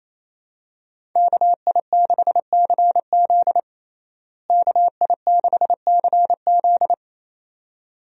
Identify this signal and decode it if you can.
Morse code, heard as a keyed tone, about 28 wpm, 700 Hz. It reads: KI6CZ KI6CZ
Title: USA Call Signs - 5 Characters - Repeated 2x - 28wpm